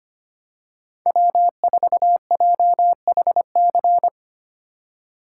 Morse code, heard as a keyed tone, about 25 wpm, 700 Hz. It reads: W4JHC
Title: USA Call Signs - 5 Characters - 3x Character Spacing - 25wpm